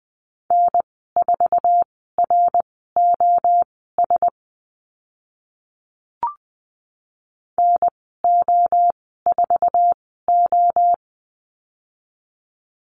Morse code, heard as a keyed tone, about 20 wpm, spaced as for 14 wpm, 700 Hz. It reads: N4ROS NO4O
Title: 069 Lesson 12g - Letter d - Call Signs 20wpm